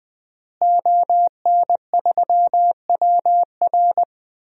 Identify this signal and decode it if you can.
Morse code, heard as a keyed tone, about 21 wpm, 700 Hz. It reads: ON3WR
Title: DXCC-100 Call Signs - 5 Characters - Repeated 3x - 20wpm